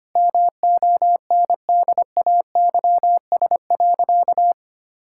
Morse code, heard as a keyed tone, about 25 wpm, 700 Hz. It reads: MONDAYS.
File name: Easy Sentences with Hints - 25wpm